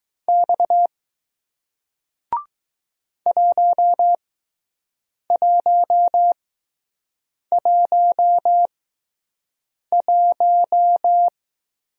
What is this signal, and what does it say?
Morse code, heard as a keyed tone, about 20 wpm, spaced as for 6 wpm, 700 Hz. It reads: X 1111